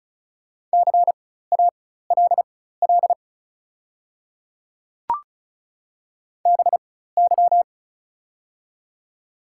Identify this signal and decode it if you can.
Morse code, heard as a keyed tone, about 36 wpm, spaced as for 16 wpm, 690 Hz. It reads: CALL BY
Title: 4x Character Spacing - Top 100 Words - 35wpm